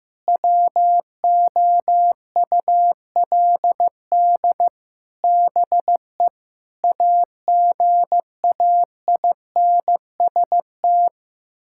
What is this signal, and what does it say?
Morse code, heard as a keyed tone, about 16 wpm, 700 Hz. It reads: WOULD BE AGAINST
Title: Sentences from Top 1000 Words - Review Part 6 - 15wpm